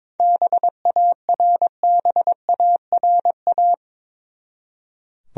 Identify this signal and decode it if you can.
Morse code, heard as a keyed tone, about 22 wpm, 700 Hz. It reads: BARBARA
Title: USA Common Names - 22wpm